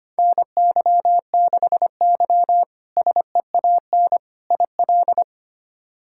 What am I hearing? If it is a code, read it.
Morse code, heard as a keyed tone, about 25 wpm, 700 Hz. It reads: NY6Y SEAN IL
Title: K1USN SST QSO Practice - 25wpm